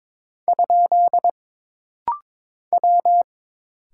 Morse code, heard as a keyed tone, about 22 wpm, 700 Hz. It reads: ? W
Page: This rapid-fire timing is geared to push your ICR (Instant Character Recognition) skills.